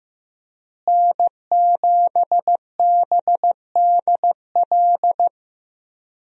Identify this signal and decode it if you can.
Morse code, heard as a keyed tone, about 15 wpm, 700 Hz. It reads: N7BDL
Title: USA Call Signs - 5 Characters - Repeated 3x - 15wpm